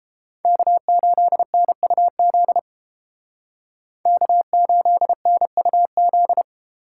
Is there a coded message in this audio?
Yes: Morse code from 0.5 to 6.4 s, heard as a keyed tone, about 31 wpm, 700 Hz. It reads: K8NUZ K8NUZ